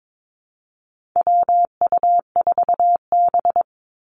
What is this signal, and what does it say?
Morse code, heard as a keyed tone, about 22 wpm, 700 Hz. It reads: WU4B
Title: Call Signs - 22wpm